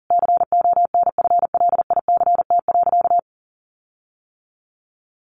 Morse code, heard as a keyed tone, about 40 wpm, 700 Hz. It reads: CONFLICT.
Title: Sentences from Top 1300 Words 40wpm